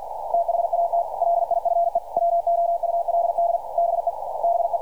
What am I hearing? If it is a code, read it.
Morse code, heard as a keyed tone, about 49 wpm, 680 Hz. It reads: THAT LIFE WOULD BE BE
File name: archived / CRSnet